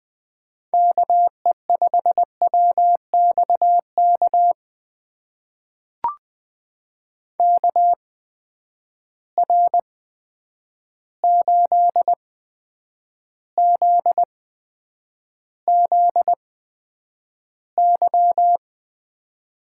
Morse code, heard as a keyed tone, about 20 wpm, spaced as for 9 wpm, 700 Hz. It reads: KE5WXK KR8ZZY
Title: USA Call Signs - 6 Characters - 8x Character Spacing - 20wpm